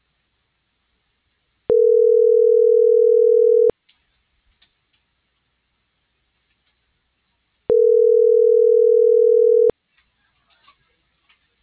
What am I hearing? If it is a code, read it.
Ringback tone (North America), 440 + 480 Hz.